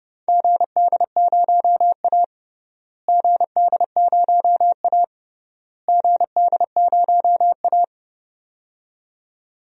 Morse code, heard as a keyed tone, about 32 wpm, 700 Hz. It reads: GD0A GD0A GD0A
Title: DXCC-100 Call Signs - 4 Characters - Repeated 3x - 30wpm